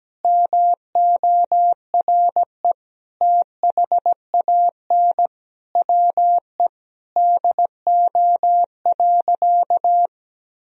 Morse code, heard as a keyed tone, about 17 wpm, 700 Hz. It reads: MORE THAN WE DO.
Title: Sentences from Top 100 Words and Abbreviations 17wpm